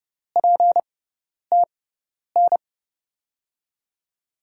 Morse code, heard as a keyed tone, about 30 wpm, spaced as for 10 wpm, 700 Hz. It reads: PTN
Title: USA Call Signs - 6x Character Spacing - 30wpm